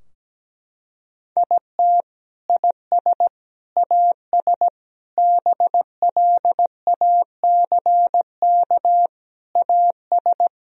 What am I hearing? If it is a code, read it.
Morse code, heard as a keyed tone, about 17 wpm, 700 Hz. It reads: IT IS AS BLACK AS
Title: Sentences from Top 1000 Words - Review Part 2 - 17wpm